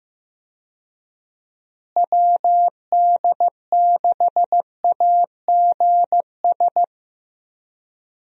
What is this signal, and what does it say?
Morse code, heard as a keyed tone, about 16 wpm, 700 Hz. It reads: WD6AGS
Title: USA Call Signs - 6 Characters - Repeated 3x - 15wpm